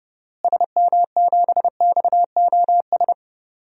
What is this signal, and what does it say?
Morse code, heard as a keyed tone, about 30 wpm, 700 Hz. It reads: SM7XOS
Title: DXCC-100 Call Signs - 6 Characters - Repeated 2x - 30wpm